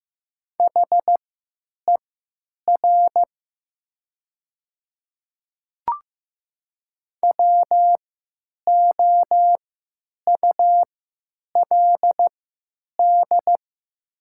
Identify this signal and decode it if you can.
Morse code, heard as a keyed tone, about 15 wpm, spaced as for 9 wpm, 700 Hz. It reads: HER WOULD
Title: 3x Character Spacing - Top 300 Words - 15wpm